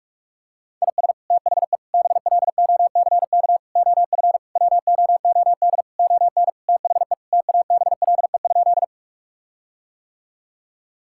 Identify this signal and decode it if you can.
Morse code, heard as a keyed tone, about 46 wpm, 700 Hz. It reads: IS THE BLOCK OF WOOD ON THE TABLE?
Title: Sentences from Top 1600 Words 45wpm